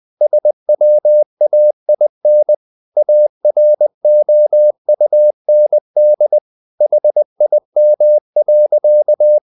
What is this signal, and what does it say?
Morse code, heard as a keyed tone, about 20 wpm, 600 Hz. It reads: SWAIN AROUND HIM.
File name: the old man and the sea part 2 20wpm